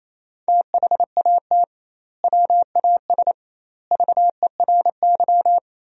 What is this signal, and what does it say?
Morse code, heard as a keyed tone, about 27 wpm, 700 Hz. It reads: THAT WAS VERY
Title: 2x Word-Spacing - Sentences from Top 200 Words - 28wpm